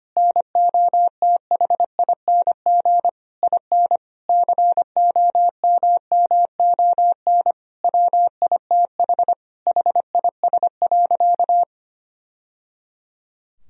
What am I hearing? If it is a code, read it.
Morse code, heard as a keyed tone, about 25 wpm, 700 Hz. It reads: NOTHING IN COMMON WITH HIS.
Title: Sentences from Top 1000 Words - Review Part 3 - 25wpm